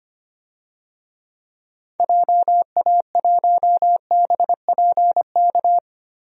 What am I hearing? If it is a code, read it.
Morse code, heard as a keyed tone, about 25 wpm, 700 Hz. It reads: JA1BPK
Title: DXCC-100 Call Signs - 6 Characters - Repeated 3x - 25wpm